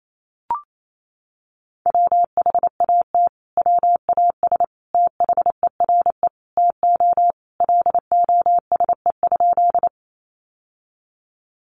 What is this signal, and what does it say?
Morse code, heard as a keyed tone, about 28 wpm, 700 Hz. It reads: WHAT WAS THERE TO LOSE?
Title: Sentences from Top 1100 Words 28wpm